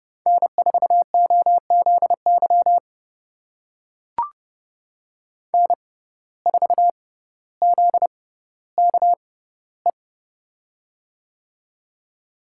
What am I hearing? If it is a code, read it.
Morse code, heard as a keyed tone, about 30 wpm, spaced as for 15 wpm, 700 Hz. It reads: N4OZY N4ZKE
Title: USA Call Signs - 5 Characters - 6x Character Spacing - 30wpm